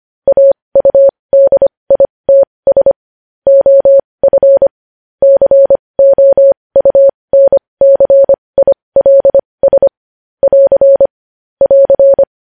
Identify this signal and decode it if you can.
Morse code, heard as a keyed tone, about 25 wpm, spaced as for 20 wpm, 550 Hz. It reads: AUDITS OF COUNCILS <AR> <AR>
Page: news-headlines-in-morse-code-at-20-wpm